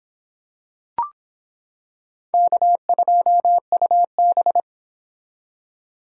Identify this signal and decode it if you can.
Morse code, heard as a keyed tone, about 25 wpm, 700 Hz. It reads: K2UB